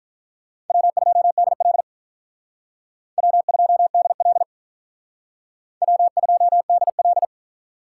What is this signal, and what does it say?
Morse code, heard as a keyed tone, about 44 wpm, 700 Hz. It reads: W2DL W2DL W2DL